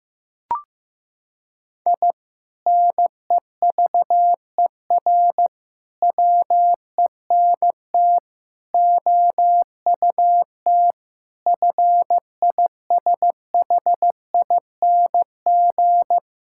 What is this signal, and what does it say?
Morse code, heard as a keyed tone, about 16 wpm, 700 Hz. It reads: I NEVER WENT OUT FISHING
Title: Sentences from Top 1000 Words - Review Part 5 - 15wpm